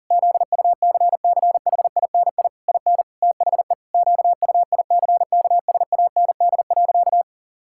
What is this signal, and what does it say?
Morse code, heard as a keyed tone, about 40 wpm, 700 Hz. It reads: ZUCCHINI IN THE QUICKSAND.
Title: Pangram Challenge v2 - 40wpm